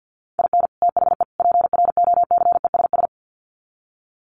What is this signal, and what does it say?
Morse code, heard as a keyed tone, about 50 wpm, 700 Hz. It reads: IN THE PROCESS
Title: Sets of 3 Words 50wpm